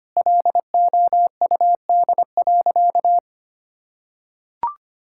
Morse code, heard as a keyed tone, about 25 wpm, 700 Hz. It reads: LOUD.
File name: practice / 25wpm / Sentences with 2-Word Phrases - 25wpm